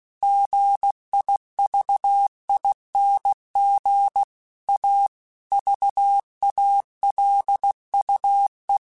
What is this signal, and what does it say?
Morse code, heard as a keyed tone, about 16 wpm, 800 Hz. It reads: GIVING A VALUE
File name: Technical Subjects / Technical Subjects-mp3s / XC @15